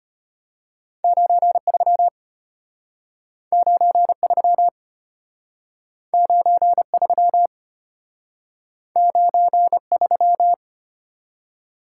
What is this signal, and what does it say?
Morse code, heard as a keyed tone, about 33 wpm, 700 Hz. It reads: 93 93 93 93